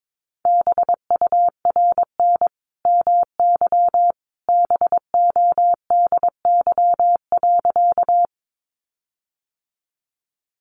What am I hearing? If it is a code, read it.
Morse code, heard as a keyed tone, about 22 wpm, 700 Hz. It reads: BURN MY BODY.
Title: Sentences from Top 1300 Words 22wpm